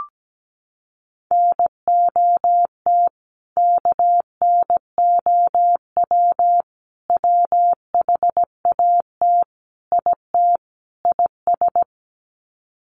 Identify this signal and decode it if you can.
Morse code, heard as a keyed tone, about 17 wpm, 700 Hz. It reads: NOT KNOW WHAT IT IS
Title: Sets of 5 Words Encore 17wpm